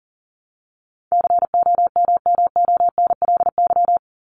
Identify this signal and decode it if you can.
Morse code, heard as a keyed tone, about 40 wpm, 700 Hz. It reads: COMMONLY
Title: LY Suffix - Top 100 Words with Base Words 40wpm